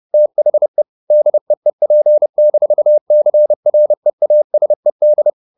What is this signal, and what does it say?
Morse code, heard as a keyed tone, about 30 wpm, 600 Hz. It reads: THE DEEP-CREASED
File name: the old man and the sea / the old man and the sea 1 30wpm